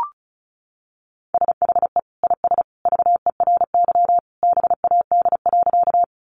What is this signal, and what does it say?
Morse code, heard as a keyed tone, about 34 wpm, 700 Hz. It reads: SHE IS VERY BAD.
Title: Sentences from Top 700 Words 35wpm